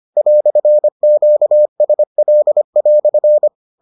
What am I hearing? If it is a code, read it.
Morse code, heard as a keyed tone, about 25 wpm, 600 Hz. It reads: "QSL"